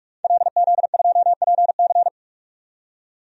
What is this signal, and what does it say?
Morse code, heard as a keyed tone, about 46 wpm, 700 Hz. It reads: LZ2PC
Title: DXCC-100 Call Signs - 5 Characters - 45wpm